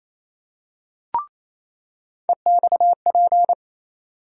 Morse code, heard as a keyed tone, about 28 wpm, 700 Hz. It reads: EXP
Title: USA Call Sign Postfixes - 3 Letters 28wpm